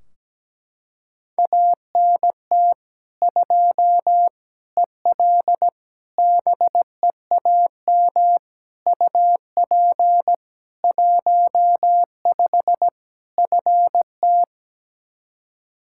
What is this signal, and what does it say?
Morse code, heard as a keyed tone, about 17 wpm, 700 Hz. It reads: ANT 2 EL BEAM UP 15 FT
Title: Practice QSO - Exchange 2 - Info - 17wpm